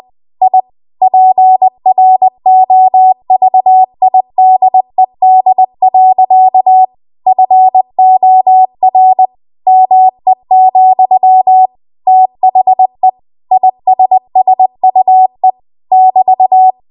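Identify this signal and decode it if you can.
Morse code, heard as a keyed tone, about 20 wpm, 750 Hz. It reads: I PROVIDED. FOR ME, THE ISSUE <BT>
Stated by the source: Morse code practice at 20 WPM.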